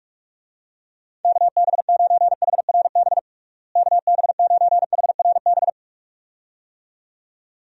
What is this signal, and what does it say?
Morse code, heard as a keyed tone, about 42 wpm, 700 Hz. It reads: KB9HRB KB9HRB
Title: USA Call Signs - 6 Characters - Repeated 2x - 45wpm